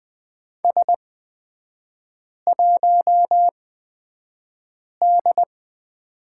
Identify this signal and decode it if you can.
Morse code, heard as a keyed tone, about 20 wpm, spaced as for 5 wpm, 700 Hz. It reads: S1D
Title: 224 Lesson 36e - BT - Three Characters 20wpm